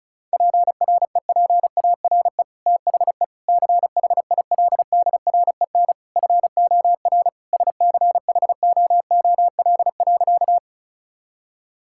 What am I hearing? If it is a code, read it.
Morse code, heard as a keyed tone, about 36 wpm, 700 Hz. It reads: PREPARE THE CHILDREN FOR SCHOOL.
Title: Sentences from Top 1000 Words - Review Part 6 - 35wpm